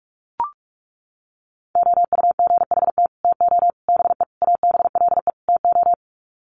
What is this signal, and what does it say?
Morse code, heard as a keyed tone, about 44 wpm, 700 Hz. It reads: OUGHT TO BE ABLE TO
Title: Sets of 5 Words Encore2 45wpm